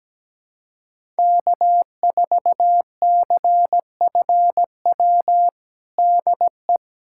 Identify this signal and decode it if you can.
Morse code, heard as a keyed tone, about 17 wpm, 700 Hz. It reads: K4CFW DE
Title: USA Callsign DE Callsign 17wpm